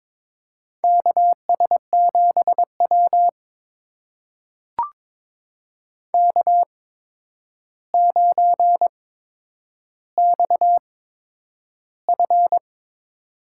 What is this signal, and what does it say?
Morse code, heard as a keyed tone, about 22 wpm, spaced as for 9 wpm, 700 Hz. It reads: KS7W K9XF
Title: USA Call Signs - 4 Characters - 8x Character Spacing - 22wpm